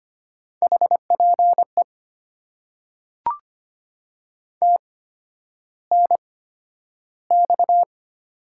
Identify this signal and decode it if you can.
Morse code, heard as a keyed tone, about 25 wpm, spaced as for 11 wpm, 700 Hz. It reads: HPE TNX
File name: Top 90 QSO Elements - 8x Character Spacing - 25wpm